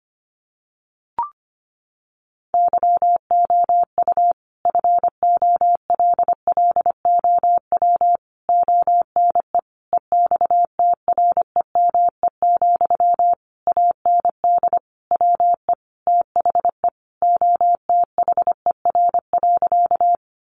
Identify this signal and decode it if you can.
Morse code, heard as a keyed tone, about 25 wpm, 700 Hz. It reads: YOU FOLLOW ONE EXTREME, AND WE THE OTHER.
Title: Sentences from Top 1300 Words 25wpm